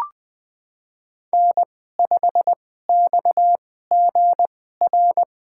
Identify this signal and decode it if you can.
Morse code, heard as a keyed tone, about 21 wpm, spaced as for 15 wpm, 700 Hz. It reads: N5XGR
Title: USA Call Signs - 5 Characters - 2x Character Spacing - 20wpm